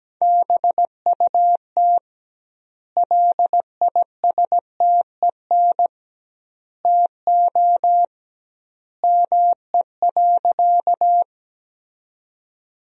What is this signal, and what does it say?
Morse code, heard as a keyed tone, about 17 wpm, 700 Hz. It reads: BUT LISTEN TO ME.
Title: 2x Word-Spacing - Sentences from Top 400 Words - 17wpm